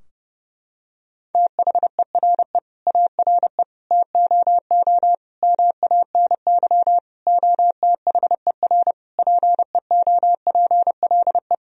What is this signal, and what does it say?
Morse code, heard as a keyed tone, about 31 wpm, 700 Hz. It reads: THERE ARE TOO MANY OTHER PEOPLE
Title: Sentences from Top 1000 Words - Review Part 3 - 30wpm